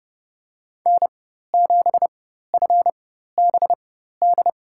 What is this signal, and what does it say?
Morse code, heard as a keyed tone, about 30 wpm, spaced as for 14 wpm, 700 Hz. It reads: N7FBD
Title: USA Call Signs - 5 Characters - 4x Character Spacing - 30wpm